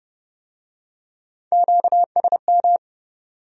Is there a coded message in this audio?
Yes: Morse code from 1.5 to 2.8 s, heard as a keyed tone, about 27 wpm, 700 Hz. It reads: QSM